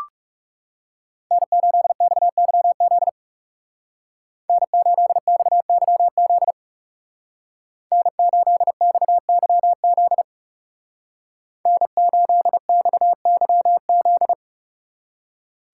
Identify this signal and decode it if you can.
Morse code, heard as a keyed tone, about 35 wpm, 700 Hz. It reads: N8XYZ N8XYZ N8XYZ N8XYZ
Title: Speed Racing - USA 1x3 Call Signs 30wpm